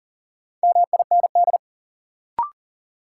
Morse code, heard as a keyed tone, about 40 wpm, 700 Hz. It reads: MIND
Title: Sets of 5 Words - New Word Drill - 40wpm